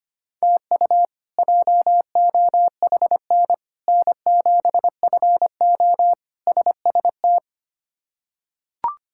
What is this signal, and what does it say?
Morse code, heard as a keyed tone, about 25 wpm, 700 Hz. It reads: TU JOHN N7FO SST
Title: K1USN SST QSO Practice - 25wpm